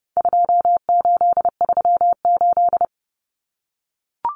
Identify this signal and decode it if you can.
Morse code, heard as a keyed tone, about 30 wpm, 700 Hz. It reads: 2838
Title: Common Cut Numbers - 4-Digit 30wpm